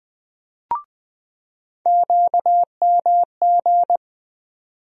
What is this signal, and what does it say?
Morse code, heard as a keyed tone, about 20 wpm, 700 Hz. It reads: QMG
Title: USA Call Sign Postfixes - 3 Letters 20wpm